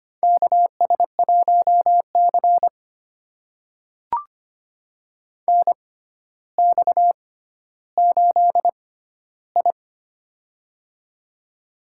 Morse code, heard as a keyed tone, about 25 wpm, spaced as for 13 wpm, 700 Hz. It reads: KS1C NX8I
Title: USA Call Signs - 4 Characters - 6x Character Spacing - 25wpm